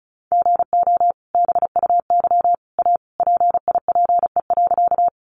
Morse code, heard as a keyed tone, about 34 wpm, 700 Hz. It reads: GO BUY A PIPE.
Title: Sentences from Top 1500 Words 35wpm